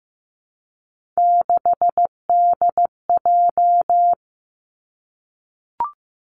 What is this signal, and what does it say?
Morse code, heard as a keyed tone, about 15 wpm, 700 Hz. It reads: 6DJ
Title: Three Challenging Letter-Number Combinations 15wpm